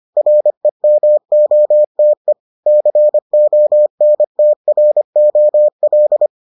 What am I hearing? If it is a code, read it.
Morse code, heard as a keyed tone, about 25 wpm, 600 Hz. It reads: REMOTE CONTROL